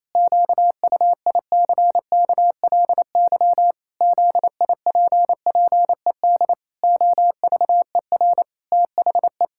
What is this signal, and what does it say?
Morse code, heard as a keyed tone, about 28 wpm, 700 Hz. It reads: QUICKLY ZIPPED OVER THE
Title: Pangram Challenge v2 - 28wpm